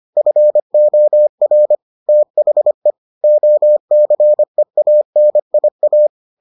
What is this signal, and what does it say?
Morse code, heard as a keyed tone, about 25 wpm, 600 Hz. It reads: FOR THE OCEANIA